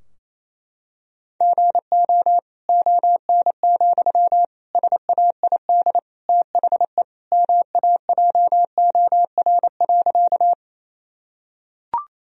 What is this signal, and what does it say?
Morse code, heard as a keyed tone, about 28 wpm, 700 Hz. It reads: GO ON, SAID THE MAJOR.
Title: Sentences from Top 1000 Words - Review Part 4 - 28wpm